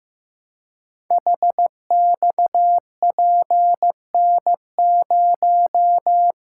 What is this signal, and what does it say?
Morse code, heard as a keyed tone, about 15 wpm, 700 Hz. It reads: HXPN0
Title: Code Groups - 5 Letter:Numbers - 15wpm